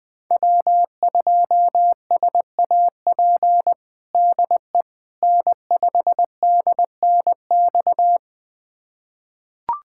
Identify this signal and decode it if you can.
Morse code, heard as a keyed tone, about 20 wpm, 700 Hz. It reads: W2SAP DE N5DNX
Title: USA Callsign DE Callsign - 5 Characters - 20wpm